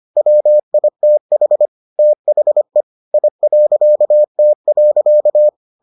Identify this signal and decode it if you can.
Morse code, heard as a keyed tone, about 25 wpm, 600 Hz. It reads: WITH THE I.T.